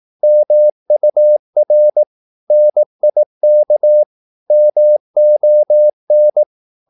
Morse code, heard as a keyed tone, about 18 wpm, 600 Hz. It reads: MUR NIK MON
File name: Korte ord DK 18wpm